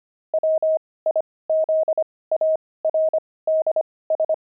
Morse code, heard as a keyed tone, about 25 wpm, spaced as for 18 wpm, 630 Hz. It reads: WIZARDS
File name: pangram-fs-18-cs-25-jumping